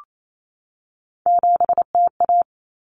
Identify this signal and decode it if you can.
Morse code, heard as a keyed tone, about 28 wpm, 700 Hz. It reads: 7TA
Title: Common Cut Numbers - 3-Digit 28wpm